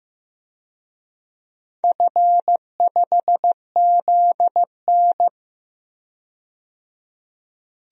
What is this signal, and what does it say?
Morse code, heard as a keyed tone, about 15 wpm, 700 Hz. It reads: F5ZN
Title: DXCC-100 Call Signs - 4 Characters - 15wpm